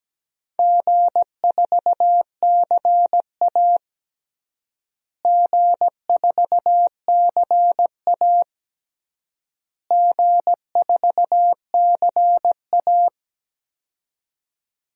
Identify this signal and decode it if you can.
Morse code, heard as a keyed tone, about 17 wpm, 700 Hz. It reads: G4CA G4CA G4CA
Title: DXCC-100 Call Signs - 4 Characters - Repeated 3x - 17wpm